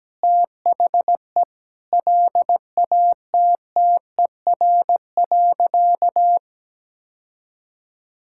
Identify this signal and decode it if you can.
Morse code, heard as a keyed tone, about 17 wpm, 700 Hz. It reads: THE LATTER.
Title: Sentences from Top 1600 Words 17wpm